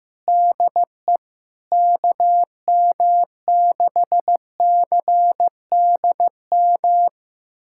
Morse code, heard as a keyed tone, about 15 wpm, 700 Hz. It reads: DE KM6CDM
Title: USA Callsign DE Callsign 15wpm